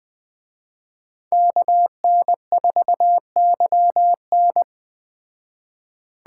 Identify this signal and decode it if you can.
Morse code, heard as a keyed tone, about 21 wpm, 700 Hz. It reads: KN4YN
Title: Speed Racing - USA Call Signs 17wpm